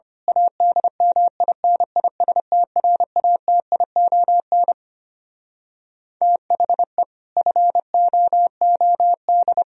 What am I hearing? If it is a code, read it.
Morse code, heard as a keyed tone, about 25 wpm, 700 Hz. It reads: ADMINISTRATION THE FOOD
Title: Speed-Racing - Sets of 5 Words 25wpm